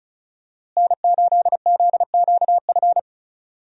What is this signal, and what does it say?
Morse code, heard as a keyed tone, about 34 wpm, 700 Hz. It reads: N8ZQF
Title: USA Call Signs - 5 Characters - Repeated 3x - 35wpm